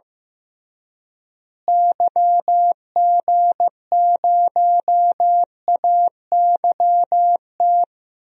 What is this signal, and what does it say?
Morse code, heard as a keyed tone, about 15 wpm, 700 Hz. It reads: YG0AYT
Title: DXCC-100 Call Signs - 6 Characters - Repeated 3x - 15wpm